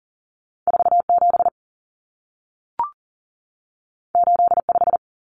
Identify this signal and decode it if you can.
Morse code, heard as a keyed tone, about 40 wpm, 700 Hz. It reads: 47 85
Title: Numbers - 2-Digit 40wpm